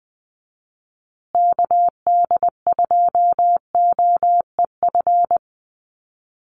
Morse code, heard as a keyed tone, about 20 wpm, 700 Hz. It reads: KD2OEF
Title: USA Call Signs Repeated 3x 20wpm